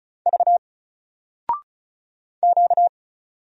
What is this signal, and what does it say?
Morse code, heard as a keyed tone, about 36 wpm, 700 Hz. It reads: V Q
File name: Single-Letter-Number-35wpm